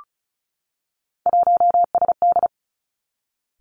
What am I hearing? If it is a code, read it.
Morse code, heard as a keyed tone, about 36 wpm, 700 Hz. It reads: 1SD
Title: Three Challenging Letter-Number Combinations 35wpm